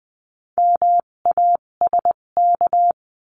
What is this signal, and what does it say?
Morse code, heard as a keyed tone, about 20 wpm, spaced as for 17 wpm, 700 Hz. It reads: MASK
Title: 257 Lesson 42b - Words 14-20wpm